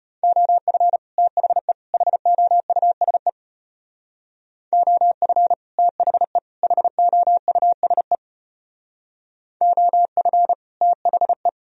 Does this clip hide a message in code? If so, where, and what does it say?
Yes: Morse code from 0.2 to 11.5 s, heard as a keyed tone, about 33 wpm, 700 Hz. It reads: OF THE HOUSE OF THE HOUSE OF THE